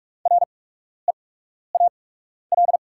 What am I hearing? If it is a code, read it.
Morse code, heard as a keyed tone, about 44 wpm, spaced as for 12 wpm, 700 Hz. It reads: REAL
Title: 8x Character Spacing - Top 300 Words - 45wpm